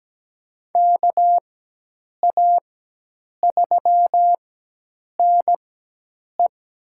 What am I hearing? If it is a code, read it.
Morse code, heard as a keyed tone, about 17 wpm, spaced as for 8 wpm, 700 Hz. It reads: KA3NE
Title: USA Call Signs - 6 Characters - 4x Character Spacing - 17wpm